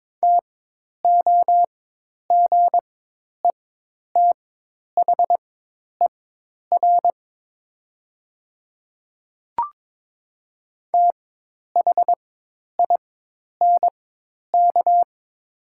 Morse code, heard as a keyed tone, about 22 wpm, spaced as for 10 wpm, 700 Hz. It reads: TOGETHER THINK
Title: 4x Character Spacing - Top 300 Words - 22wpm